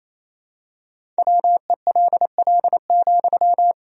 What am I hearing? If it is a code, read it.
Morse code, heard as a keyed tone, about 28 wpm, 700 Hz. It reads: WELL,
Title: Sentences from Top 500 Words and Abbreviations 28wpm